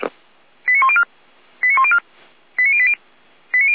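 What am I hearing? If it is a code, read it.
Five-tone selective call (ZVEI), heard as quick runs of five tones, tones of about 70 ms. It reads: 89104 89104 89080
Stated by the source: Received on an SDR; ZVEI five-tone call 89330.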